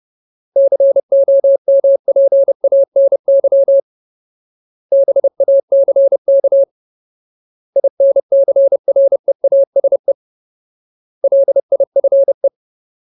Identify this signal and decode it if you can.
Morse code, heard as a keyed tone, about 30 wpm, 550 Hz. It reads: COMPANY BACK INCREASE LIFE
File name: Ultimate Word List / 30WPM Ultimate